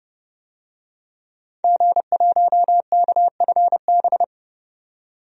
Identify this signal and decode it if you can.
Morse code, heard as a keyed tone, about 32 wpm, 700 Hz. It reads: G1KFB
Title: DXCC-100 Call Signs - 5 Characters - 30wpm